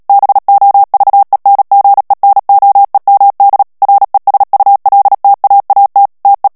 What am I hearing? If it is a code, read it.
Morse code, heard as a keyed tone, about 37 wpm, 800 Hz. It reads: BOVENGENOEMD RESULTAAT TE
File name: morse mp3 / HOLLANDAIS texte clair 2 vitesse 35 mots minute